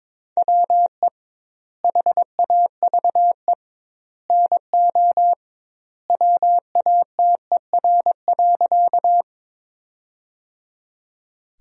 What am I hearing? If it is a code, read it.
Morse code, heard as a keyed tone, about 22 wpm, 700 Hz. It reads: WE HAVE NO WATER.
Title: 2x Word-Spacing - Sentences from Top 100 Words - 22wpm